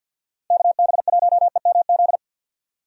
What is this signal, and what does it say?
Morse code, heard as a keyed tone, about 50 wpm, 700 Hz. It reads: KB1EMZ
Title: USA Call Signs - 6 Characters - 50wpm